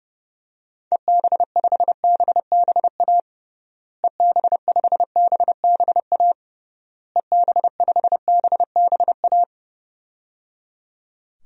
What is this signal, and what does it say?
Morse code, heard as a keyed tone, about 30 wpm, 700 Hz. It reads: EB5BBA EB5BBA EB5BBA
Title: DXCC-100 Call Signs - 6 Characters - Repeated 3x - 30wpm